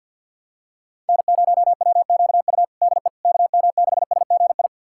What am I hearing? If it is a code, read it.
Morse code, heard as a keyed tone, about 51 wpm, 690 Hz. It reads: N0WQU DE KM6SGI
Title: USA Callsign DE Callsign 50wpm